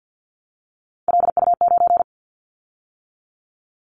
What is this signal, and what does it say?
Morse code, heard as a keyed tone, about 50 wpm, 700 Hz. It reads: LU9
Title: Callsign Prefixes for Top 100 DXCC - 50wpm